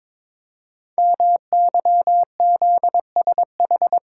Morse code, heard as a keyed tone, about 22 wpm, 700 Hz. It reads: MYZSH
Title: Code Groups - 5 Letter:Numbers - 22wpm